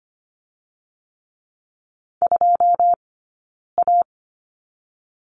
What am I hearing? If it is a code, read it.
Morse code, heard as a keyed tone, about 25 wpm, 700 Hz. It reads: 2 A
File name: course / 25:10wpm / 092 Lesson 16d - Letter c - Two Characters 25wpm